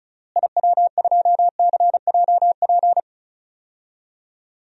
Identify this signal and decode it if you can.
Morse code, heard as a keyed tone, about 36 wpm, 700 Hz. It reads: IW2CJP
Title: DXCC-100 Call Signs - 6 Characters - Repeated 3x - 35wpm